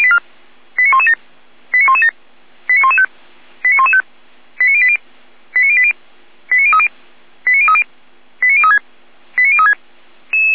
Five-tone selective call (ZVEI), heard as quick runs of five tones, tones of about 70 ms. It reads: 89117 89117 89115 89115 89080 89080 89030 89030 89036 89036